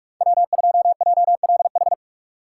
Morse code, heard as a keyed tone, about 43 wpm, 700 Hz. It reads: W2JLH
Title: USA Call Signs - 5 Characters - 2x Character Spacing - 45wpm